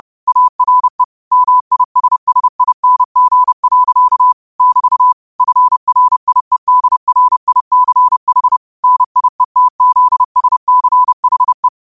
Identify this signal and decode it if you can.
Morse code, heard as a keyed tone, about 30 wpm, 1000 Hz. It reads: ARE MISSING. <BT> FRIEDRICH NIETZSCHE
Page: New quotes every day in morse code at 30 Words per minute.